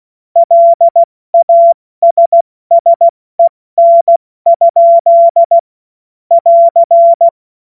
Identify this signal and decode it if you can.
Morse code, heard as a keyed tone, about 16 wpm, 680 Hz. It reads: LASSEN? <AR>
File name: Morsekurs / Witze / Witz 02. Mai 2023 70 F80